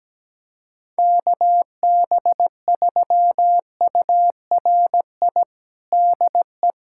Morse code, heard as a keyed tone, about 17 wpm, 700 Hz. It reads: KB3URI DE
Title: USA Callsign DE Callsign 17wpm